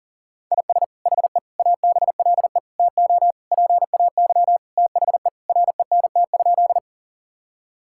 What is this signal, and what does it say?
Morse code, heard as a keyed tone, about 37 wpm, 700 Hz. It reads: IS HE ABLE TO PAY THE RENT?
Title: Sentences from Top 1000 Words - Review Part 5 - 40wpm